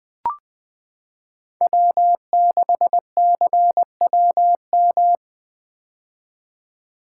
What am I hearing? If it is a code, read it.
Morse code, heard as a keyed tone, about 21 wpm, 700 Hz. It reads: W6CWM
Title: K1USN SST QSO Practice - 20wpm